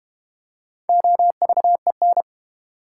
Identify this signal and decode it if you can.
Morse code, heard as a keyed tone, about 32 wpm, 700 Hz. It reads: OVEN